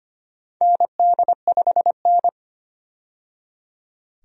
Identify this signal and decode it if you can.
Morse code, heard as a keyed tone, about 25 wpm, 700 Hz. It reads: ND5N
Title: USA Call Signs - 4 Characters - 25wpm